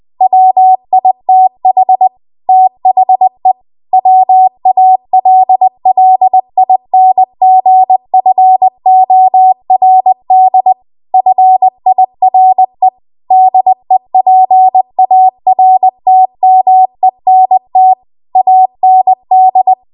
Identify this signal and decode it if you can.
Morse code, heard as a keyed tone, about 20 wpm, 750 Hz. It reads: WITH THE WALLINGFORD FIRE DEPARTMENT AND
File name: Morse / 20WPM